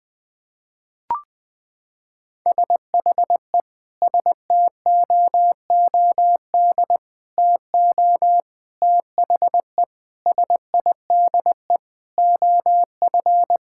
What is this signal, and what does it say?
Morse code, heard as a keyed tone, about 21 wpm, 700 Hz. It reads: SHE STOOD TO THE SIDE OF
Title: Sentences with 5-Word Phrases - 20wpm